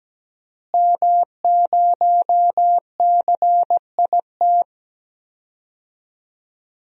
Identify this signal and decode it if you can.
Morse code, heard as a keyed tone, about 17 wpm, 700 Hz. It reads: M0CIT
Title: DXCC-100 Call Signs - 5 Characters - Repeated 3x - 17wpm